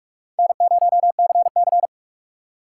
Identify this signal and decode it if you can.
Morse code, heard as a keyed tone, about 42 wpm, 700 Hz. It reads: N0CC